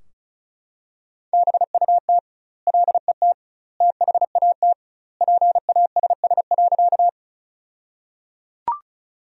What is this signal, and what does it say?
Morse code, heard as a keyed tone, about 36 wpm, 700 Hz. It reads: BUT LET THAT PASS.
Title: 2x Word-Spacing - Sentences from Top 400 Words - 35wpm